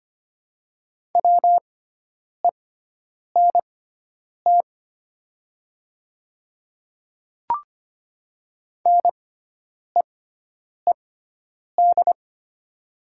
Morse code, heard as a keyed tone, about 25 wpm, spaced as for 9 wpm, 700 Hz. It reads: WENT NEED
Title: 6x Character Spacing - Top 500 Words - 25wpm